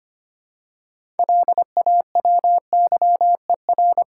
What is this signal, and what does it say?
Morse code, heard as a keyed tone, about 25 wpm, 700 Hz. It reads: LAWYER
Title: Six Letter Words - 25wpm